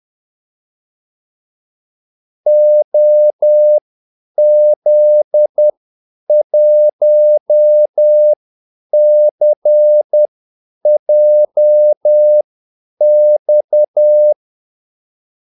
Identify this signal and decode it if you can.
Morse code, heard as a keyed tone, about 10 wpm, spaced as for 8 wpm, 600 Hz. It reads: OZ1CJX
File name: Callsigns-10-8wpm